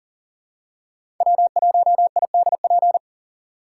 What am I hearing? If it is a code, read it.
Morse code, heard as a keyed tone, about 40 wpm, 700 Hz. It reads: W1IDP
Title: USA Call Signs - 2x Character Spacing - 40wpm